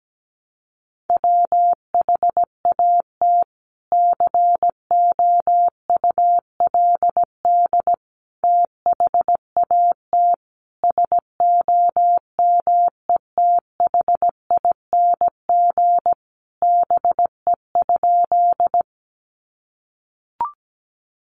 Morse code, heard as a keyed tone, about 17 wpm, 700 Hz. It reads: WHAT COULD THAT SOMETHING BE?
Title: Sentences from Top 600 Words 17wpm